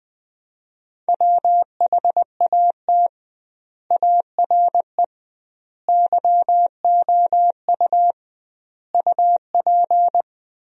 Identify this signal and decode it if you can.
Morse code, heard as a keyed tone, about 20 wpm, 700 Hz. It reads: WHAT ARE YOU UP
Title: 2x Word-Spacing - Sentences from Top 100 Words - 20wpm